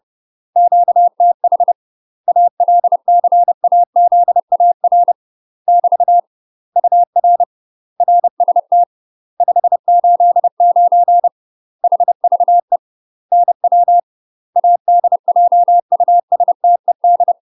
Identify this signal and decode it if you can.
Morse code, heard as a keyed tone, about 30 wpm, 700 Hz. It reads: QTH ALCAZAR <BT> UR RST 589 HVE NW ADJUSTED
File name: UK MASTER QSOs TRAINING / 30WPM / UK MASTER QSOs 0026